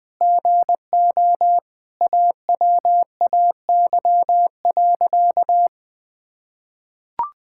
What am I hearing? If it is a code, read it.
Morse code, heard as a keyed tone, about 20 wpm, 700 Hz. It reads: GO AWAY.
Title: Sentences with 4-Word Phrases - 20wpm